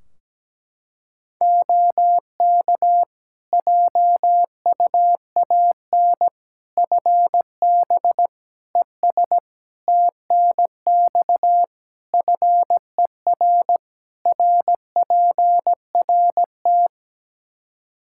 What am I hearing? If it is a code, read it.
Morse code, heard as a keyed tone, about 17 wpm, 700 Hz. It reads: OK JUAN FB ES TNX FER RPRT
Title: K1USN SST QSO Practice 17wpm